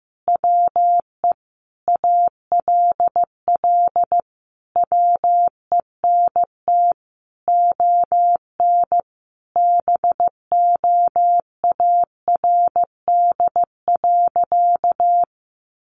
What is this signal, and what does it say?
Morse code, heard as a keyed tone, about 14 wpm, 700 Hz. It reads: WE ALL WENT ON BOARD.
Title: Sentences from Top 1400 Words 15wpm